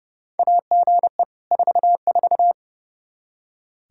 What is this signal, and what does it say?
Morse code, heard as a keyed tone, about 30 wpm, 700 Hz. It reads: AGE 44
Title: Practice QSO - Exchange 3 - More Info - 30wpm